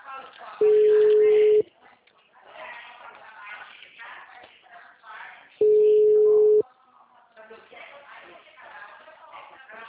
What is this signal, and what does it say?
Ringback tone (Europe, 425 Hz), with speech.
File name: out- -2003- - - .74197